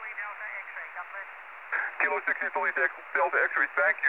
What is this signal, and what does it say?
Emergency Alert System (SAME) tones, heard as harsh buzzing bursts, with speech. It is heard from 0.4 to 3.1 s.